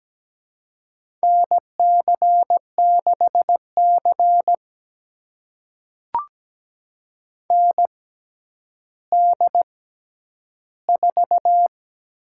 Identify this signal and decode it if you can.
Morse code, heard as a keyed tone, about 17 wpm, spaced as for 10 wpm, 700 Hz. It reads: NC6C ND4
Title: USA Call Signs - 6x Character Spacing - 17wpm